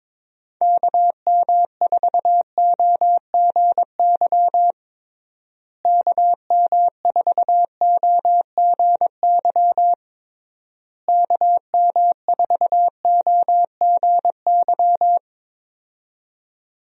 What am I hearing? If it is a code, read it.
Morse code, heard as a keyed tone, about 22 wpm, 700 Hz. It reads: KM4OGY KM4OGY KM4OGY
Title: USA Call Signs - 6 Characters - Repeated 3x - 22wpm